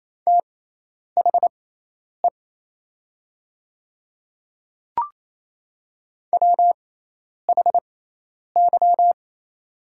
Morse code, heard as a keyed tone, about 27 wpm, spaced as for 10 wpm, 700 Hz. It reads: THE WHY
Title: 6x Character Spacing - Top 500 Words - 28wpm